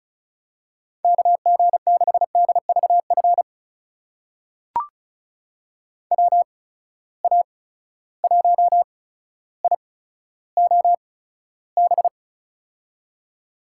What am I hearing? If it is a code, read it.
Morse code, heard as a keyed tone, about 34 wpm, spaced as for 15 wpm, 700 Hz. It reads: KG6DVF WA1IOB
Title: USA Call Signs - 6 Characters - 8x Character Spacing - 35wpm